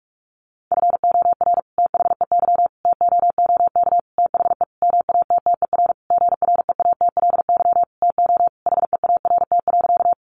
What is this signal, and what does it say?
Morse code, heard as a keyed tone, about 44 wpm, 700 Hz. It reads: FOR THEY TOOK THE MATTER GREATLY TO HEART.
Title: Sentences from Top 1400 Words 45wpm